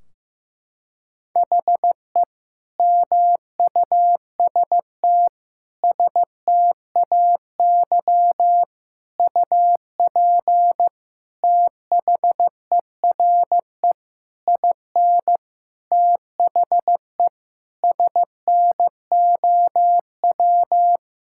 Morse code, heard as a keyed tone, about 15 wpm, 700 Hz. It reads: HE MUST STAY UP THERE IN THE SNOW
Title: Sentences from Top 1600 Words 15wpm